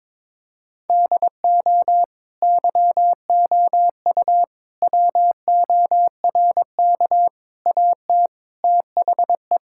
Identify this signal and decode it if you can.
Morse code, heard as a keyed tone, about 22 wpm, 700 Hz. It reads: DO YOU WORK AT THE
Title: Sentences from Top 1000 Words - Review Part 5 - 22wpm